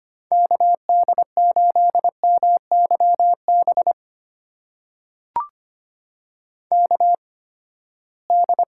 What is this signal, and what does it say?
Morse code, heard as a keyed tone, about 25 wpm, spaced as for 17 wpm, 700 Hz. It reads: KD8MYB KD
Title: USA Call Signs - 6 Characters - 8x Character Spacing - 25wpm